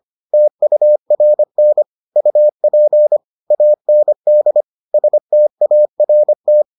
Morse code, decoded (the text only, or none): TURN UP AND START